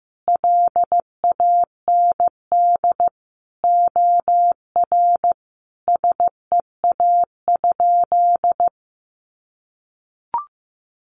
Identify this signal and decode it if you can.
Morse code, heard as a keyed tone, about 15 wpm, 700 Hz. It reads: LAND OR SEA?
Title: Sentences from Top 500 Words 15wpm